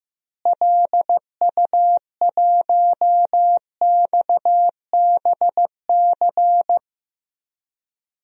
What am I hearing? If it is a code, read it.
Morse code, heard as a keyed tone, about 15 wpm, 700 Hz. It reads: LU1XBC
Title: DXCC-100 Call Signs - 6 Characters - Repeated 3x - 15wpm